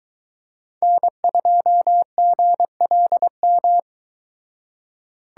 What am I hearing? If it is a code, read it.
Morse code, heard as a keyed tone, about 24 wpm, 700 Hz. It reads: N2GLM